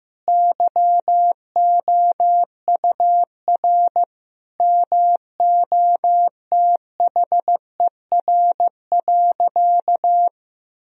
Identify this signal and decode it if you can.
Morse code, heard as a keyed tone, about 16 wpm, 700 Hz. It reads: YOUR MOTHER.
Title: Sentences from Top 1000 Words - Review Part 4 - 15wpm